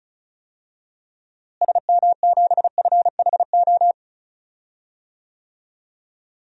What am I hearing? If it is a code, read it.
Morse code, heard as a keyed tone, about 36 wpm, 700 Hz. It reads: SM7FHO
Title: DXCC-100 Call Signs - 6 Characters - Repeated 3x - 35wpm